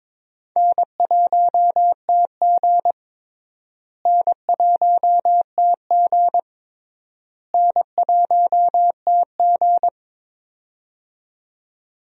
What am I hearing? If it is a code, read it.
Morse code, heard as a keyed tone, about 23 wpm, 700 Hz. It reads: N1TG N1TG N1TG
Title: USA Call Signs - 4 Characters - Repeated 3x - 22wpm